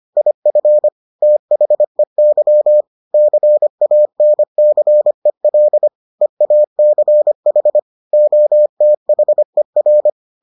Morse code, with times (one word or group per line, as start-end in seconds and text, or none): 0.2-0.9 IF
1.2-2.8 THEY
3.1-5.9 CANCEL
6.2-7.8 EACH
8.1-10.1 OTHER